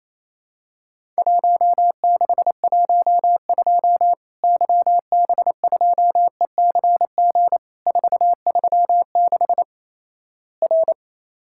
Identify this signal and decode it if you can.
Morse code, heard as a keyed tone, about 28 wpm, 700 Hz. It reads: 1612 YB2ECG 436 R
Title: Worked All Europe DX Contest - QTC Exchange - 28wpm